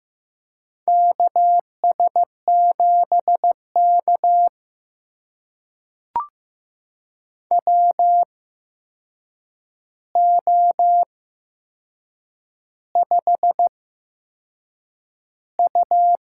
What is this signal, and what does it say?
Morse code, heard as a keyed tone, about 15 wpm, spaced as for 6 wpm, 700 Hz. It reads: KS7K WO5U
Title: USA Call Signs - 4 Characters - 8x Character Spacing - 15wpm